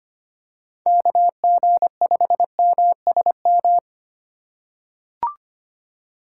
Morse code, heard as a keyed tone, about 25 wpm, 700 Hz. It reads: KG5MSM
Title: USA Call Signs - 2x Character Spacing - 25wpm